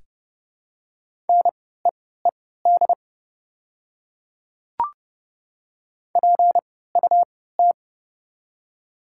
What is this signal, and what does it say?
Morse code, heard as a keyed tone, about 31 wpm, spaced as for 17 wpm, 690 Hz. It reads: NEED PUT
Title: 3x Character Spacing - Top 200 Words - 30wpm